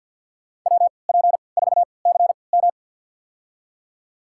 Morse code, heard as a keyed tone, about 53 wpm, spaced as for 29 wpm, 700 Hz. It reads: WP4CM
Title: USA Call Signs - 5 Characters - 3x Character Spacing - 50wpm